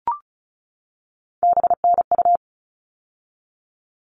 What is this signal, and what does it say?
Morse code, heard as a keyed tone, about 34 wpm, 700 Hz. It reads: BNU
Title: All Cut Numbers - 3-Digit 35wpm